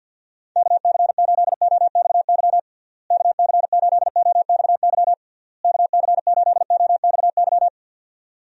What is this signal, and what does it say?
Morse code, heard as a keyed tone, about 52 wpm, 700 Hz. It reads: KC8OXY KC8OXY KC8OXY
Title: USA Call Signs - 6 Characters - Repeated 3x - 50wpm